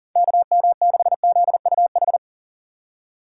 Morse code, heard as a keyed tone, about 40 wpm, 700 Hz. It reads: KM6ZUH
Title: USA Call Signs - 6 Characters - 4x Character Spacing - 40wpm